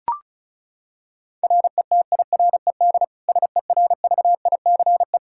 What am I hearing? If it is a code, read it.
Morse code, heard as a keyed tone, about 36 wpm, 700 Hz. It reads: RETIRED SERVICE
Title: Practice QSO - Exchange 3 - More Info - 35wpm